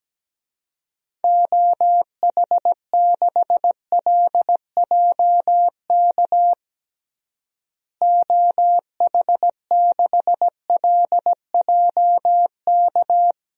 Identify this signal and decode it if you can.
Morse code, heard as a keyed tone, about 17 wpm, 700 Hz. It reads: OH6LJK OH6LJK
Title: DXCC-100 Call Signs - 6 Characters - Repeated 2x - 17wpm